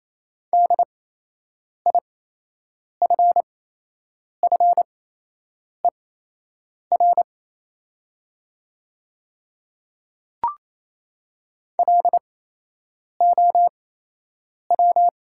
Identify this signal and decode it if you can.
Morse code, heard as a keyed tone, about 28 wpm, spaced as for 8 wpm, 700 Hz. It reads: DIFFER LOW
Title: 8x Character Spacing - Top 200 Words - 28wpm